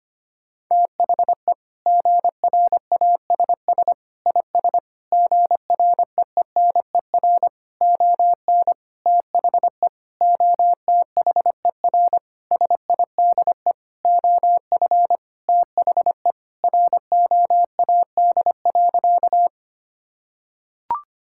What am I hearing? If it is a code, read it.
Morse code, heard as a keyed tone, about 25 wpm, 700 Hz. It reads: THE GRASS IS GREENER ON THE OTHER SIDE OF THE ROAD.
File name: practice / 25wpm / Sentences with 4-Word Phrases - 25wpm